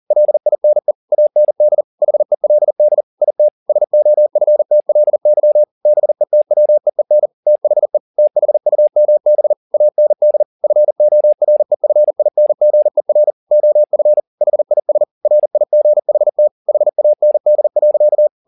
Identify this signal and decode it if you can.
Morse code, heard as a keyed tone, about 40 wpm, 600 Hz. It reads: LINE AND HELD IT SOFTLY BETWEEN THE THUMB AND FOREFINGER OF HIS RIGHT HAND.